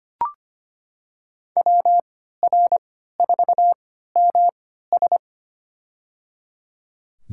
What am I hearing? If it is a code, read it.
Morse code, heard as a keyed tone, about 25 wpm, spaced as for 14 wpm, 700 Hz. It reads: WR4MS
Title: USA Call Signs - 5 Characters - 3x Character Spacing - 25wpm